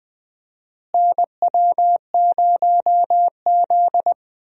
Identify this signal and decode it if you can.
Morse code, heard as a keyed tone, about 20 wpm, 700 Hz. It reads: NW0Z
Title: USA Call Signs - 4 Characters - Repeated 3x - 20wpm